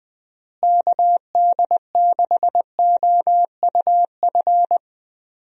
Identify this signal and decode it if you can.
Morse code, heard as a keyed tone, about 21 wpm, 700 Hz. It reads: KD6OUF
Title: USA Call Signs - 6 Characters - Repeated 3x - 20wpm